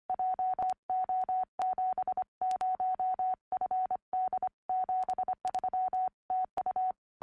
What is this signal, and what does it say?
Morse code, heard as a keyed tone, about 24 wpm, 750 Hz. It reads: JO70FD 73 TU
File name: ea3d 4m